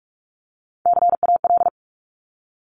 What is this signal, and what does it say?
Morse code, heard as a keyed tone, about 46 wpm, 700 Hz. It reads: CAL
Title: Top Three Letter Combinations in Words 45wpm